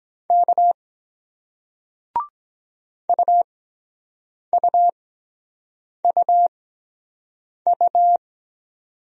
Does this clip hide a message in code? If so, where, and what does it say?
Yes: Morse code from 0.3 to 8.2 s, heard as a keyed tone, about 23 wpm, spaced as for 7 wpm, 700 Hz. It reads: K UUUU